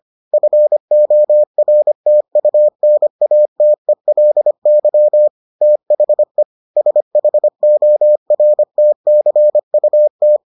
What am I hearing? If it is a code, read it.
Morse code, heard as a keyed tone, about 25 wpm, 600 Hz. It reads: FORTUNATELY THE SHORTCUT